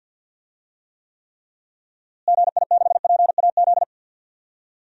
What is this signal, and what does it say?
Morse code, heard as a keyed tone, about 51 wpm, 700 Hz. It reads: GI6PAZ
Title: DXCC-100 Call Signs - 6 Characters - Repeated 3x - 50wpm